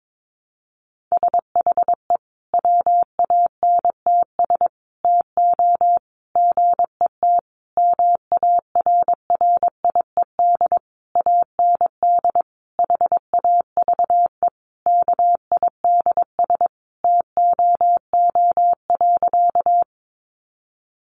Morse code, heard as a keyed tone, about 22 wpm, 700 Hz. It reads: SHE WANTS TO GET MARRIED AND HAVE KIDS TOO.
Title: Easy Phrases 22wpm